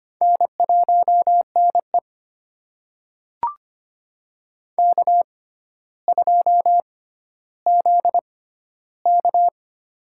Morse code, heard as a keyed tone, about 25 wpm, spaced as for 13 wpm, 700 Hz. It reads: N1NE K2ZK
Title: USA Call Signs - 4 Characters - 6x Character Spacing - 25wpm